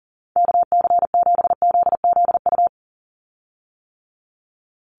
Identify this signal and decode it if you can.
Morse code, heard as a keyed tone, about 39 wpm, 700 Hz. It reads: KC7ZZU
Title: USA Call Signs 40wpm